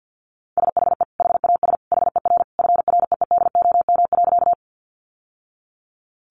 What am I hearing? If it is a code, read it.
Morse code, heard as a keyed tone, about 50 wpm, 700 Hz. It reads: SHE HAS HER FREEDOM.
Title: Sentences from Top 800 Words 50wpm